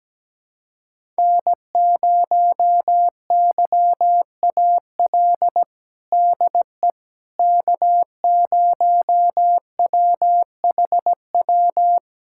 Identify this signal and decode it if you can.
Morse code, heard as a keyed tone, about 17 wpm, 700 Hz. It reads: N0YAL DE K0WHW
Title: USA Callsign DE Callsign - 5 Characters - 17wpm